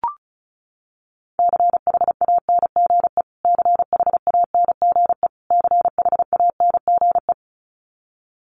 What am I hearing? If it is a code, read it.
Morse code, heard as a keyed tone, about 34 wpm, 700 Hz. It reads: CHANGE CHANGE CHANGE
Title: Top 500 Words Repeated 35wpm